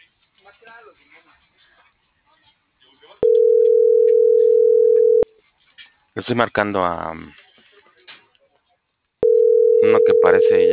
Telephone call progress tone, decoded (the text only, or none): ringback tone